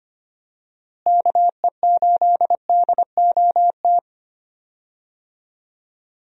Morse code, heard as a keyed tone, about 25 wpm, 700 Hz. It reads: KE8DOT
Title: USA Call Signs - 6 Characters - Repeated 3x - 25wpm